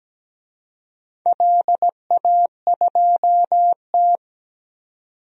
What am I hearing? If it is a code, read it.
Morse code, heard as a keyed tone, about 17 wpm, 700 Hz. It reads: LA2T
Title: DXCC-100 Call Signs - 4 Characters - 17wpm